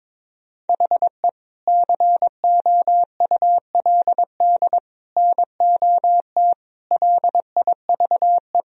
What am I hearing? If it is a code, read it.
Morse code, heard as a keyed tone, about 22 wpm, 700 Hz. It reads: HE COULD NOT LIVE
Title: Sentences from Top 1000 Words - Review Part 3 - 22wpm